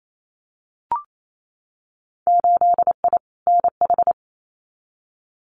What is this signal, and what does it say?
Morse code, heard as a keyed tone, about 28 wpm, 700 Hz. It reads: 8I NH
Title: Winter Field Day Exchanges Encore 28wpm